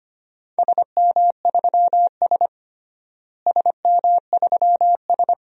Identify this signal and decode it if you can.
Morse code, heard as a keyed tone, about 25 wpm, 700 Hz. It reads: SM3S SM3S
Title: DXCC-100 Call Signs - 4 Characters - Repeated 3x - 25wpm